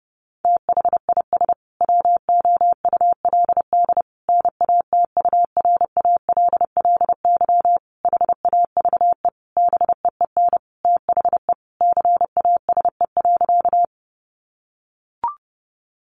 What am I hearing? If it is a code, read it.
Morse code, heard as a keyed tone, about 30 wpm, 700 Hz. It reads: THIS WOULD NATURALLY HAVE BEEN THE CASE.
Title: Sentences from Top 1300 Words 30wpm